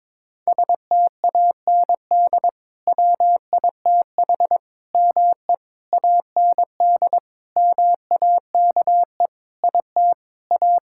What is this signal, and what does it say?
Morse code, heard as a keyed tone, about 22 wpm, 700 Hz. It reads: STAND WITH ME AND MAKE IT A
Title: Sentences from Top 1600 Words 22wpm